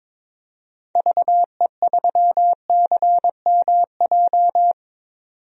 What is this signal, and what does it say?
Morse code, heard as a keyed tone, about 22 wpm, 700 Hz. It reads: VE3CMJ
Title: DXCC-100 Call Signs - 6 Characters - 22wpm